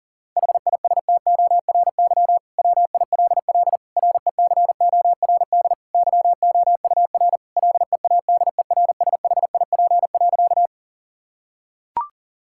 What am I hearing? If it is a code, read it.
Morse code, heard as a keyed tone, about 40 wpm, 700 Hz. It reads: HISTORY WILL RECORD YOUR LEADERSHIP.
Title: Sentences from Top 1600 Words 40wpm